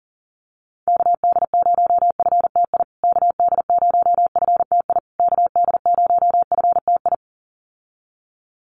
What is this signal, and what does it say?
Morse code, heard as a keyed tone, about 40 wpm, 700 Hz. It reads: KD0FTI KD0FTI KD0FTI
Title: USA Call Signs Repeated 3x 40wpm